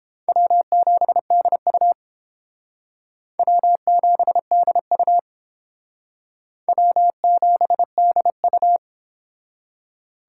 Morse code, heard as a keyed tone, about 28 wpm, 700 Hz. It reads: W7DU W7DU W7DU